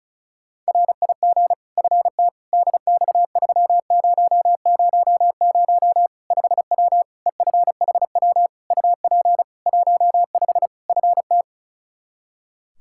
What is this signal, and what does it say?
Morse code, heard as a keyed tone, about 36 wpm, 700 Hz. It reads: RIG FT DX3000 5W EFHW UP 15 FT
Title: K1USN SST QSO Practice 35wpm